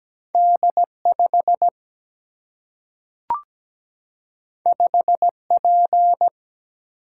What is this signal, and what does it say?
Morse code, heard as a keyed tone, about 17 wpm, 700 Hz. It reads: D 5 5 P
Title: Single-Letter Single-Number 17wpm